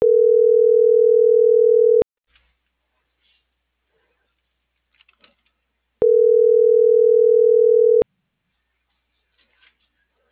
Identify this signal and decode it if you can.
Ringback tone (North America), 440 + 480 Hz.